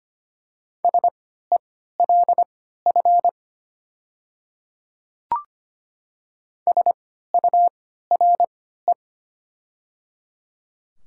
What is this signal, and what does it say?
Morse code, heard as a keyed tone, about 25 wpm, spaced as for 14 wpm, 700 Hz. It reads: SELF SURE
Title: 3x Character Spacing - Top 500 Words - 25wpm